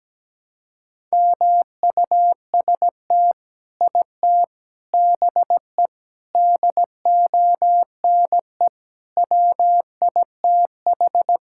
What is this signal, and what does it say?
Morse code, heard as a keyed tone, about 17 wpm, 700 Hz. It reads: MUST IT BE DONE WITH
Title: Sentences from Top 1000 Words - Review Part 5 - 17wpm